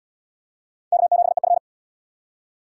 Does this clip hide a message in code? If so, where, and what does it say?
Yes: Morse code from 0.9 to 1.6 s, heard as a keyed tone, about 76 wpm, 700 Hz. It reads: K7EF